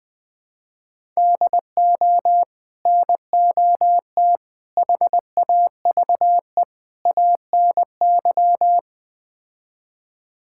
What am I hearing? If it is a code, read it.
Morse code, heard as a keyed tone, about 20 wpm, 700 Hz. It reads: DO NOT HAVE ANY
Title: Build A Phrase - 4-Word Phrases - 20wpm